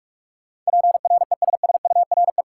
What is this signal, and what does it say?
Morse code, heard as a keyed tone, about 46 wpm, 700 Hz. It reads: PRESSURE
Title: Sentences from Top 1000 Words - Review Part 6 - New Word Drill - 45wpm